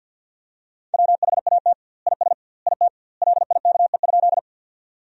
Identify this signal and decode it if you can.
Morse code, heard as a keyed tone, about 50 wpm, 700 Hz. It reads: WHAT IS IT LIKE?
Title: 2x Word-Spacing - Sentences from Top 100 Words - 50wpm